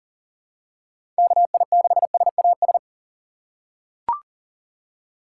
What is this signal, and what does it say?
Morse code, heard as a keyed tone, about 40 wpm, 700 Hz. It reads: KI6SAS
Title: USA Call Signs - 6 Characters - 3x Character Spacing - 40wpm